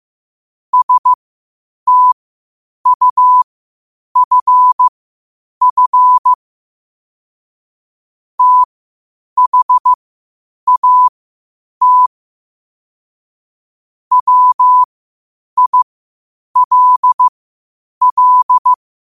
Morse code, heard as a keyed tone, about 15 wpm, spaced as for 8 wpm, 1000 Hz. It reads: STUFF THAT WILL
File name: qotdfiles / 8wpm